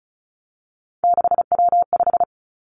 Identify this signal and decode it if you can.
Morse code, heard as a keyed tone, about 36 wpm, 700 Hz. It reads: 6W5
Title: Three Challenging Letter-Number Combinations 35wpm